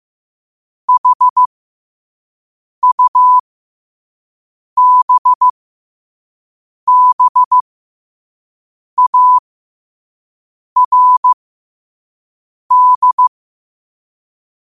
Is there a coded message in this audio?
Yes: Morse code from 0.9 to 13.3 s, heard as a keyed tone, about 15 wpm, spaced as for 5 wpm, 1000 Hz. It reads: HUBBARD